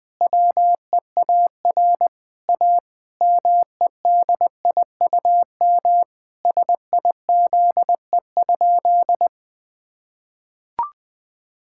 Morse code, heard as a keyed tone, about 21 wpm, 700 Hz. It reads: WEAR A MEDIUM SIZE?
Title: Sentences from Top 1000 Words - Review Part 6 - 20wpm